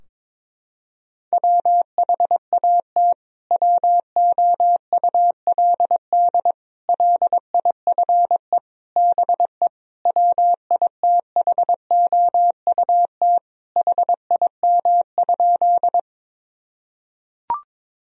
Morse code, heard as a keyed tone, about 21 wpm, 700 Hz. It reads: WHAT WOULD LIFE BE WITHOUT HIM?
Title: Sentences from Top 1000 Words - Review Part 3 - 22wpm